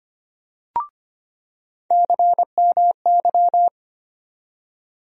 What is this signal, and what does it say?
Morse code, heard as a keyed tone, about 25 wpm, 700 Hz. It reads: CMY
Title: USA Call Sign Postfixes - 3 Letters 25wpm